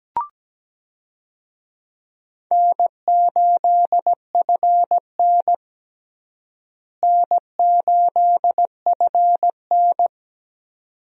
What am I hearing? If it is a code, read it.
Morse code, heard as a keyed tone, about 17 wpm, 700 Hz. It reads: N8FN N8FN
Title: USA Call Signs - 4 Characters - Repeated 2x - 17wpm